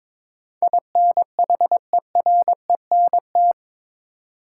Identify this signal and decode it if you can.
Morse code, heard as a keyed tone, about 22 wpm, 700 Hz. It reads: INHERENT
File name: practice / 22wpm / Eight Letter Words - 22wpm